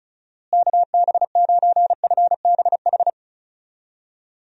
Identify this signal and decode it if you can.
Morse code, heard as a keyed tone, about 36 wpm, 700 Hz. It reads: KB9FBH
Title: USA Call Signs - 6 Characters - Repeated 2x - 35wpm